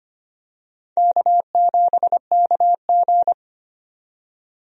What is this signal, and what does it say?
Morse code, heard as a keyed tone, about 25 wpm, 700 Hz. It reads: K7KG
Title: USA Call Signs - 4 Characters - 25wpm